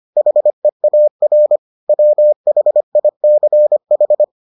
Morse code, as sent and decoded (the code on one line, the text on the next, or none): .... .- .-. / .-- .... .. -.-. ....
HEAR WHICH